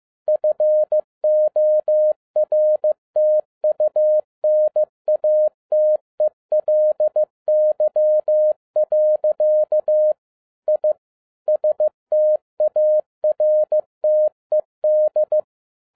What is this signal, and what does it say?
Morse code, heard as a keyed tone, about 15 wpm, 610 Hz. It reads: FORTUNATELY. I STARTED
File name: MY CW LIFE 1 15wpm